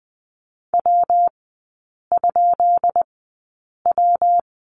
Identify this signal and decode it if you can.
Morse code, heard as a keyed tone, about 20 wpm, spaced as for 8 wpm, 700 Hz. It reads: W?W
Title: 133 Lesson 22e - Question Mark - Three Characters 20wpm